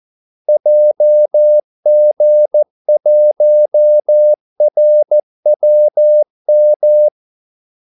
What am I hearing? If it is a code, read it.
Morse code, heard as a keyed tone, about 14 wpm, 600 Hz. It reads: JG1RWM
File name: Callsigns-14wpm